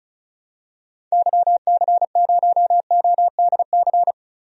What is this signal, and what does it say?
Morse code, heard as a keyed tone, about 34 wpm, 700 Hz. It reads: YC0ODC
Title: DXCC-100 Call Signs - 6 Characters - 35wpm